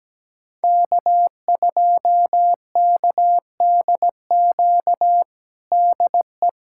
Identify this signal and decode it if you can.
Morse code, heard as a keyed tone, about 17 wpm, 700 Hz. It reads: K2KDQ DE
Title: USA Callsign DE Callsign - 5 Characters - 17wpm